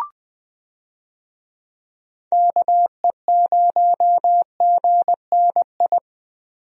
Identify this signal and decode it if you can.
Morse code, heard as a keyed tone, about 20 wpm, 700 Hz. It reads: KE0GNI
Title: USA Call Signs - 6 Characters - Repeated 2x - 20wpm